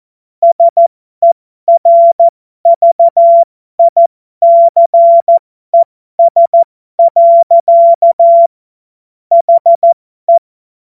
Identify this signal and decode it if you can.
Morse code, heard as a keyed tone, about 14 wpm, spaced as for 12 wpm, 680 Hz. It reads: SERVICES. HE
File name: Morsekurs / Witze